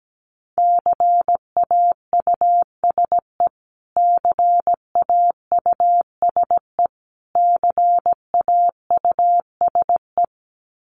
Morse code, heard as a keyed tone, about 17 wpm, 700 Hz. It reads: CAUSE CAUSE CAUSE
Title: Top 500 Words Repeated 17wpm